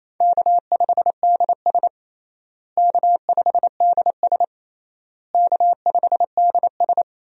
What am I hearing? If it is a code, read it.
Morse code, heard as a keyed tone, about 28 wpm, 700 Hz. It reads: K5DS K5DS K5DS
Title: USA Call Signs - 4 Characters - Repeated 3x - 28wpm